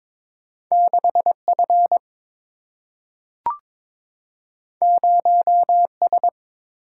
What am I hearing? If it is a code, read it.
Morse code, heard as a keyed tone, about 22 wpm, 700 Hz. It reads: 6F 0S
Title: Single-Letter Single-Number 22wpm